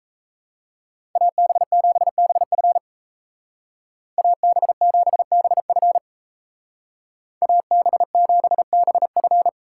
Morse code, heard as a keyed tone, about 34 wpm, 700 Hz. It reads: AB7BF AB7BF AB7BF
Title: Speed Racing - USA 2x2 Call Signs 28wpm